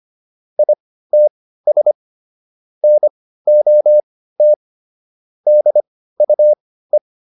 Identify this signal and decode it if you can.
Morse code, heard as a keyed tone, about 25 wpm, spaced as for 15 wpm, 600 Hz. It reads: ITS NOT DUE